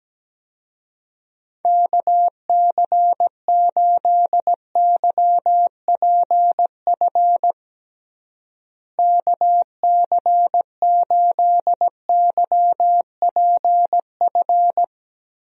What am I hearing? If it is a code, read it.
Morse code, heard as a keyed tone, about 17 wpm, 700 Hz. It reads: KC8YPF KC8YPF
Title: USA Call Signs - 6 Characters - Repeated 3x - 17wpm